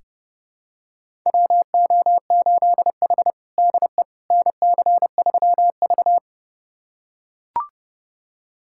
Morse code, heard as a keyed tone, about 30 wpm, 700 Hz. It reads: WO8H DE NC3V
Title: USA Callsign DE Callsign - 4 Characters - 30wpm